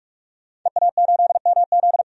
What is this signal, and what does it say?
Morse code, heard as a keyed tone, about 46 wpm, 700 Hz. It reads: EA8MZ
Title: DXCC-100 Call Signs - 5 Characters - 45wpm